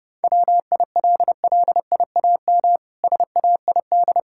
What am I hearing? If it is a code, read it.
Morse code, heard as a keyed tone, about 30 wpm, 700 Hz. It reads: WILLIAM SAID
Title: Pangram Challenge v2 - 30wpm